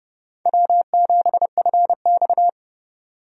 Morse code, heard as a keyed tone, about 30 wpm, 700 Hz. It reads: W7FX
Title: USA Call Signs - 4 Characters - Repeated 3x - 30wpm